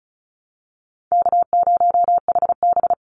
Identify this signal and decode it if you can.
Morse code, heard as a keyed tone, about 34 wpm, 700 Hz. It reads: K0HB
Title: Easy Call Signs Repeated 3x - 35wpm